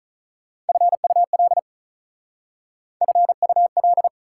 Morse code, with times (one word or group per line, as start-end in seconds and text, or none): 0.7-1.6 FUL
3.0-4.1 FUL